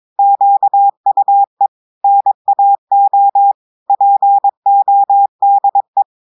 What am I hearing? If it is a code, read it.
Morse code, heard as a keyed tone, about 22 wpm, 800 Hz. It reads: QUE NAO PODE
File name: audios / portugais / portugais texte en clair 1 vitesse 22 mots minutes